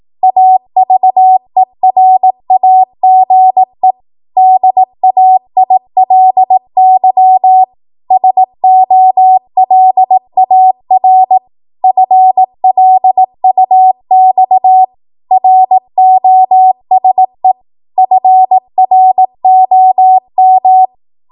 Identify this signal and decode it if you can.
Morse code, heard as a keyed tone, about 18 wpm, 750 Hz. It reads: AVERAGE DAILY SOLAR FLUX ROSE FROM
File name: Archive / 18 WPM / 18WPM